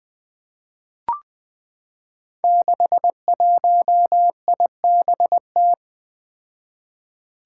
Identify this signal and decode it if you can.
Morse code, heard as a keyed tone, about 21 wpm, 700 Hz. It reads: 61IBT
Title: Five Challenging Letter-Number Combinations 20wpm